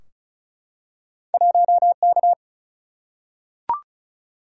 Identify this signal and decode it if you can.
Morse code, heard as a keyed tone, about 36 wpm, 700 Hz. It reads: 1K
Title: Single-Letter Single-Number 35wpm